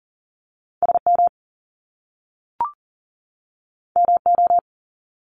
Morse code, heard as a keyed tone, about 40 wpm, 700 Hz. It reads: SM MO
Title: Top Two Letter Combinations in Words 40wpm